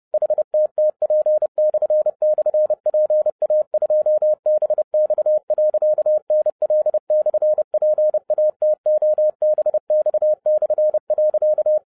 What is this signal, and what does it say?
Morse code, heard as a keyed tone, about 30 wpm, 610 Hz. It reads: HTTP//PA2BX.NL/PATOBX/.